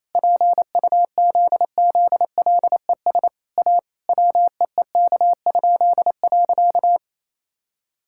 Morse code, heard as a keyed tone, about 28 wpm, 700 Hz. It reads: PUZZLES A WEEK?.
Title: Pangram Challenge v2 - 28wpm